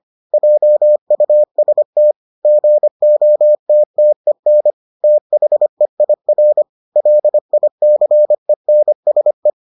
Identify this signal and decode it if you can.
Morse code, heard as a keyed tone, about 25 wpm, 600 Hz. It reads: JUST GOTTEN THEIR LICENSE